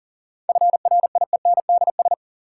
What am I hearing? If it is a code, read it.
Morse code, heard as a keyed tone, about 38 wpm, 700 Hz. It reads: FRIENDS
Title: Sentences from Top 1600 Words - New Word Drill Alt - 40wpm